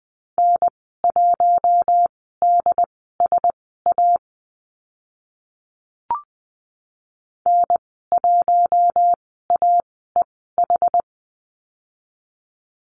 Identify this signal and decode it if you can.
Morse code, heard as a keyed tone, about 20 wpm, spaced as for 14 wpm, 700 Hz. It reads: N1DSA N1AEH
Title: 069 Lesson 12g - Letter d - Call Signs 20wpm